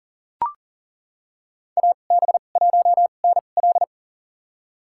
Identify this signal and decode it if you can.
Morse code, heard as a keyed tone, about 37 wpm, spaced as for 28 wpm, 700 Hz. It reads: AB1NL
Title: USA Call Signs - 2x Character Spacing - 40wpm